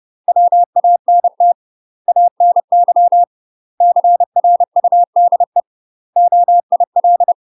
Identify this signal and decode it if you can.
Morse code, heard as a keyed tone, about 30 wpm, 700 Hz. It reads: WANT ANY CRUDE OIL
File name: UK MASTER QSOs TRAINING / 30WPM / UK MASTER QSOs 0022